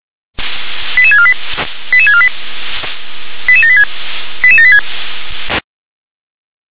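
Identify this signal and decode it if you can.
Five-tone selective call (ZVEI), heard as quick runs of five tones, tones of about 70 ms. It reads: 88648 88648 88786 88786
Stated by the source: Received on SDR, ZVEI five-tone selective call.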